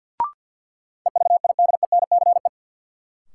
Morse code, heard as a keyed tone, about 51 wpm, 700 Hz. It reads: EVIDENCE
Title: Sentences from Top 1300 Words - New Word Drill - 50wpm